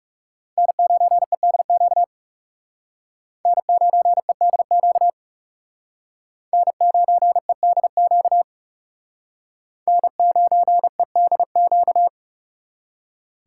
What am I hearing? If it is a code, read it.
Morse code, heard as a keyed tone, about 39 wpm, 700 Hz. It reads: N9EDQ N9EDQ N9EDQ N9EDQ